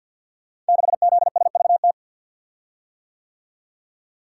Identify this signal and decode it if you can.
Morse code, heard as a keyed tone, about 51 wpm, 700 Hz. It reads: 6ZSVT
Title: Code Groups - 5 Letter:Numbers - 50wpm